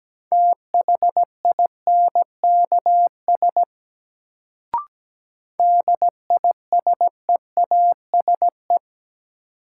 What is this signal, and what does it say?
Morse code, heard as a keyed tone, about 17 wpm, 700 Hz. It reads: THINKS DISEASE
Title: Sentences from Top 1000 Words - Review Part 6 - New Word Drill - 17wpm